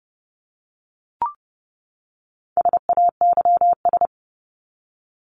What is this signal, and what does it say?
Morse code, heard as a keyed tone, about 30 wpm, 700 Hz. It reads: SAYS
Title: ES:S Suffix - Top 100 Words with Base Words 30wpm